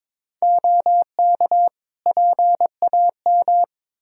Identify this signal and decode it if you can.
Morse code, heard as a keyed tone, about 22 wpm, 700 Hz. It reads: OK PAM
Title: Practice QSO - Exchange 3 - More Info - 22wpm